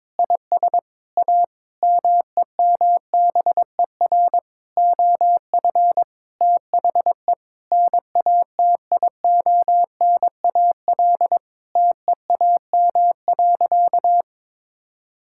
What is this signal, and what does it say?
Morse code, heard as a keyed tone, about 21 wpm, 700 Hz. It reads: IS A MEMBER OF THE NATIONAL TEAM.
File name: practice / 22wpm / Sentences with 3-Word Phrases - 22wpm